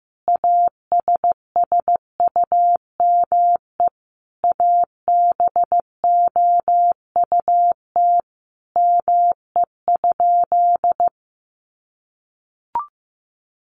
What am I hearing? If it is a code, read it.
Morse code, heard as a keyed tone, about 15 wpm, 700 Hz. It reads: ASSUME ABOUT ME?
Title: Sentences from Top 1500 Words 15wpm